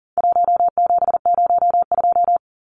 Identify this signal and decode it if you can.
Morse code, heard as a keyed tone, about 40 wpm, 700 Hz. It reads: 1702
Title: Numbers - 4-Digit 40wpm